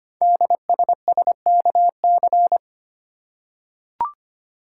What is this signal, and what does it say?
Morse code, heard as a keyed tone, about 25 wpm, 700 Hz. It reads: DSSKC
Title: Code Groups - 5 Letters - 25wpm